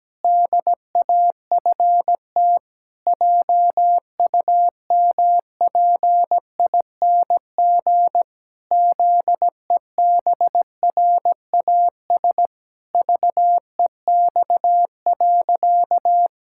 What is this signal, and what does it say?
Morse code, heard as a keyed tone, about 17 wpm, 700 Hz. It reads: DAFT JUMPING ZEBRAS VEX.
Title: Pangram Challenge v2 - 17wpm